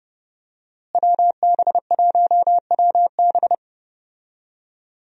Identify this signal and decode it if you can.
Morse code, heard as a keyed tone, about 30 wpm, 700 Hz. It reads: WB1WB
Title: USA Call Signs - 5 Characters - Repeated 2x - 30wpm